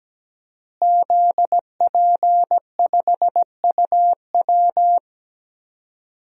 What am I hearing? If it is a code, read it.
Morse code, heard as a keyed tone, about 17 wpm, 700 Hz. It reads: ZP5UW
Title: DXCC-100 Call Signs - 5 Characters - Repeated 2x - 17wpm